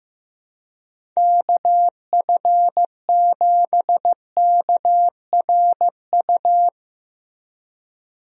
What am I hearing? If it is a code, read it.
Morse code, heard as a keyed tone, about 15 wpm, 700 Hz. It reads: KF7KRU
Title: USA Call Signs - 6 Characters - Repeated 3x - 15wpm